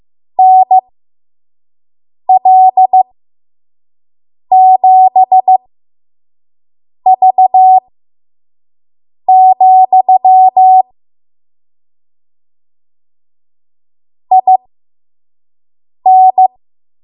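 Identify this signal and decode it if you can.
Morse code, heard as a keyed tone, about 15 wpm, spaced as for 5 wpm, 750 Hz. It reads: NL7V, IN
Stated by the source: 5 WPM Morse code practice.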